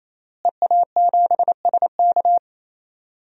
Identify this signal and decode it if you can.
Morse code, heard as a keyed tone, about 28 wpm, 690 Hz. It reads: EA7SK
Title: DXCC-100 Call Signs - 5 Characters - Repeated 3x - 28wpm